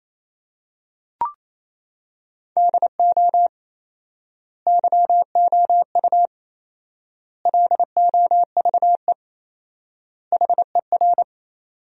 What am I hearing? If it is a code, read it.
Morse code, heard as a keyed tone, about 28 wpm, 700 Hz. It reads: DO YOU LOVE HER
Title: 4x Word-Spacing - Sentences from Top 400 Words - 28wpm